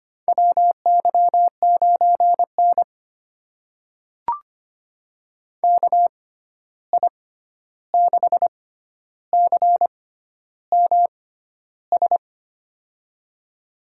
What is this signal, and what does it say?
Morse code, heard as a keyed tone, about 25 wpm, spaced as for 11 wpm, 700 Hz. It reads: WY9N KI6CMS
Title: USA Call Signs - 6x Character Spacing - 25wpm